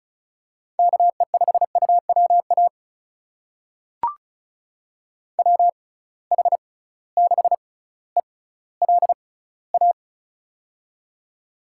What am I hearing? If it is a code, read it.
Morse code, heard as a keyed tone, about 34 wpm, spaced as for 18 wpm, 700 Hz. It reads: KE5UWA WH6ELA
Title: USA Call Signs - 6 Characters - 6x Character Spacing - 35wpm